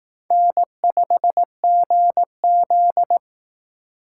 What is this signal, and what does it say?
Morse code, heard as a keyed tone, about 18 wpm, 700 Hz. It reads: N5GZ
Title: Speed Racing - USA 1x2 Call Signs 15wpm